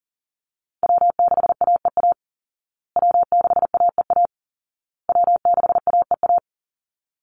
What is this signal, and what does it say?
Morse code, heard as a keyed tone, about 40 wpm, 700 Hz. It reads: W6AEA W6AEA W6AEA
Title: Call Signs Repeated 2x - 40wpm